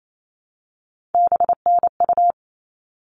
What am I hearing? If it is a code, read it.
Morse code, heard as a keyed tone, about 28 wpm, 700 Hz. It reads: BNU
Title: All Cut Numbers - 3-Digit 28wpm